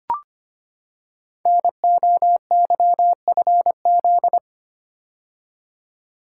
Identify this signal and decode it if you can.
Morse code, heard as a keyed tone, about 25 wpm, 700 Hz. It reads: NOYFZ
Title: Code Groups - 5 Letters - 25wpm